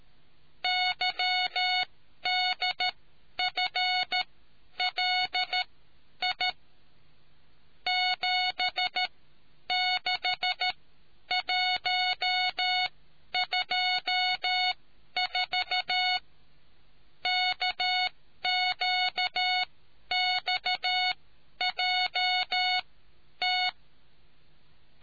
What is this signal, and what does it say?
Morse code, heard as a keyed tone, about 13 wpm, spaced as for 10 wpm, 730 Hz. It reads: YDFLI 76124 KQXJT